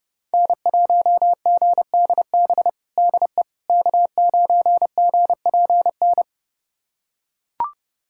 Morse code, heard as a keyed tone, about 29 wpm, 700 Hz. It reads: N1GDB DE K9GPN
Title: USA Callsign DE Callsign - 5 Characters - 30wpm